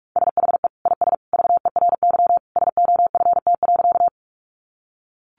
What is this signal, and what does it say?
Morse code, heard as a keyed tone, about 44 wpm, 700 Hz. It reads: SHE IS VERY SOFT.
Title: Easy Phrases 45wpm